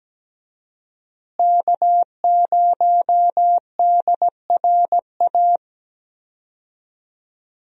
Morse code, heard as a keyed tone, about 17 wpm, 700 Hz. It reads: K0DRA
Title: USA Call Signs - 5 Characters - Repeated 2x - 17wpm